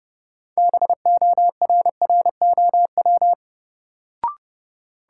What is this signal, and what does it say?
Morse code, heard as a keyed tone, about 32 wpm, 700 Hz. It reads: BORROW
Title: Sentences from Top 1500 Words - New Word Drill - 30wpm